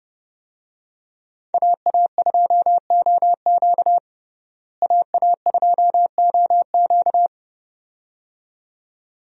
Morse code, heard as a keyed tone, about 32 wpm, 700 Hz. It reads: AA2OQ AA2OQ
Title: USA Call Signs - 5 Characters - Repeated 2x - 30wpm